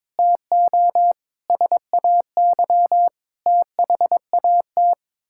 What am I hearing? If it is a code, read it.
Morse code, heard as a keyed tone, about 22 wpm, 700 Hz. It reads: TO SAY THAT
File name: practice / 22wpm / Sets of 7 Words - 22wpm